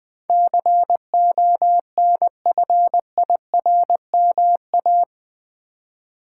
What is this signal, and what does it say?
Morse code, heard as a keyed tone, about 20 wpm, 700 Hz. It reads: CONFIRMA
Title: Build A Word - 4 Letters At A Time - 12-Letter Words - 20wpm